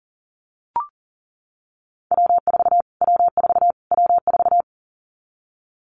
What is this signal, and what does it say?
Morse code, heard as a keyed tone, about 40 wpm, 700 Hz. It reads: W4 W4 W4
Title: Callsign Prefixes for Top 25 DXCC - Repeated 3x - 40wpm